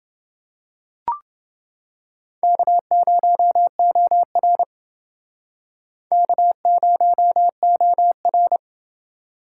Morse code, heard as a keyed tone, about 28 wpm, 700 Hz. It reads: K0OR K0OR